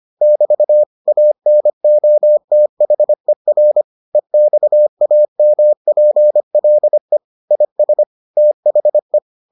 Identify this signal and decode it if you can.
Morse code, heard as a keyed tone, about 25 wpm, 600 Hz. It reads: <BT> ANOTHER EXAMPLE IS THE